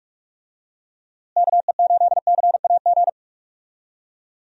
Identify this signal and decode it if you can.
Morse code, heard as a keyed tone, about 46 wpm, 700 Hz. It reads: KE8CAG
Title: USA Call Signs - 6 Characters - Repeated 2x - 45wpm